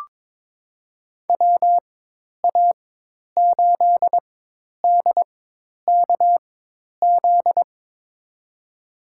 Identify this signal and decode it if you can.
Morse code, heard as a keyed tone, about 22 wpm, spaced as for 10 wpm, 700 Hz. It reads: WA8DKZ
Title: USA Call Signs - 6 Characters - 4x Character Spacing - 22wpm